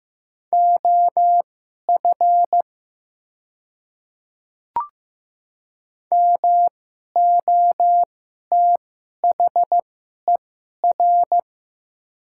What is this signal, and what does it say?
Morse code, heard as a keyed tone, about 15 wpm, spaced as for 11 wpm, 690 Hz. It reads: OF MOTHER
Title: 2x Character Spacing - Top 200 Words - 15wpm